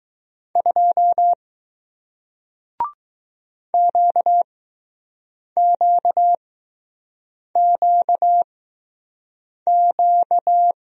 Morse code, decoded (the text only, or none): 2 QQQQ